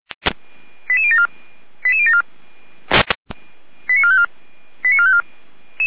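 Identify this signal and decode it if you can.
Five-tone selective call (ZVEI), heard as quick runs of five tones, tones of about 70 ms. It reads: 88074 88074 89454 89454
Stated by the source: ZVEI five-tone selective call 88103 received on SDR by fire brigade Orlamünde (feuerwehr-orlamuende).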